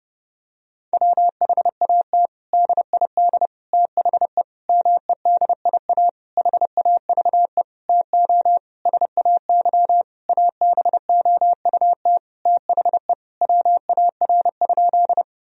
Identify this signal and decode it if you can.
Morse code, heard as a keyed tone, about 30 wpm, 700 Hz. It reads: WHAT DID THE MEDIA HAVE TO SAY ABOUT THE WAR?
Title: Sentences from Top 1000 Words - Review Part 4 - 30wpm